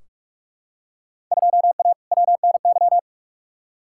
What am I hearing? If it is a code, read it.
Morse code, heard as a keyed tone, about 43 wpm, 700 Hz. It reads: 2A WNY
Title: Field Day Exchanges - 45wpm